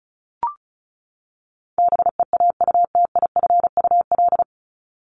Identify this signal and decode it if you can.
Morse code, heard as a keyed tone, about 34 wpm, 700 Hz. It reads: BEAUTIFUL
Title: Top 751-1000 Words 35wpm